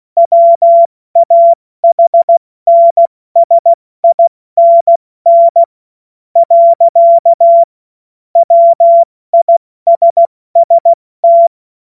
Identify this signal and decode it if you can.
Morse code, heard as a keyed tone, about 16 wpm, 680 Hz. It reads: WAHNSINN . WISST
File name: Morsekurs / Witze / Witz 02 April 2024 70 F80